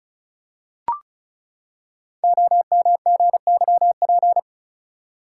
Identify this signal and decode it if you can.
Morse code, heard as a keyed tone, about 36 wpm, 690 Hz. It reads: OMGYP
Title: Code Groups - 5 Letters - 35wpm